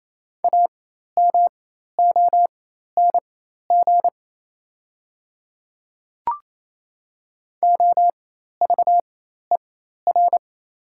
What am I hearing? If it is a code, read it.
Morse code, heard as a keyed tone, about 28 wpm, spaced as for 13 wpm, 700 Hz. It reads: AMONG OVER
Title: 4x Character Spacing - Top 500 Words - 28wpm